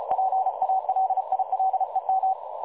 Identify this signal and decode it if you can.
Morse code, heard as a keyed tone, about 33 wpm, 760 Hz, with radio noise. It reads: PALILEM
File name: dlazaawansowanych / nagraniazpasma / websdr recording 3565.0kHz